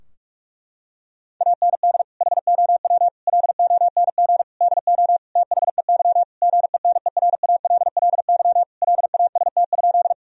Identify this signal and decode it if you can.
Morse code, heard as a keyed tone, about 41 wpm, 700 Hz. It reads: AND HOW LONG DO THEY GENERALLY LAST?
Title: Sentences from Top 1000 Words - Review Part 4 - 45wpm